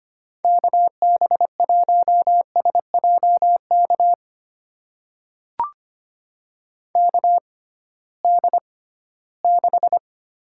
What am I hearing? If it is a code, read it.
Morse code, heard as a keyed tone, about 25 wpm, spaced as for 16 wpm, 700 Hz. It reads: KB1SJK KD6
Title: USA Call Signs - 6 Characters - 6x Character Spacing - 25wpm